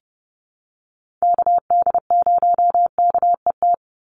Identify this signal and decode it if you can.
Morse code, heard as a keyed tone, about 30 wpm, 700 Hz. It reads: KD0KET
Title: USA Call Signs 30wpm